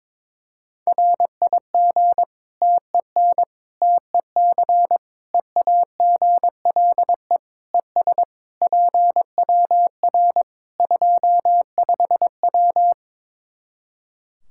Morse code, heard as a keyed tone, about 21 wpm, 700 Hz. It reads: RIG TEN TEC EAGLE ES PWR 25W
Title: K1USN SST QSO Practice 22wpm